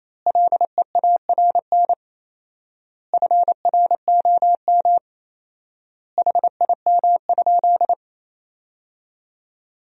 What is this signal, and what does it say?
Morse code, heard as a keyed tone, about 28 wpm, 700 Hz. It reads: LEARN FROM HIM?
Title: 4x Word-Spacing - Sentences from Top 300 Words - 28wpm